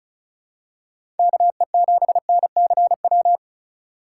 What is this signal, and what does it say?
Morse code, heard as a keyed tone, about 36 wpm, 700 Hz. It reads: KE7NCW
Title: USA Call Signs - 6 Characters - 3x Character Spacing - 35wpm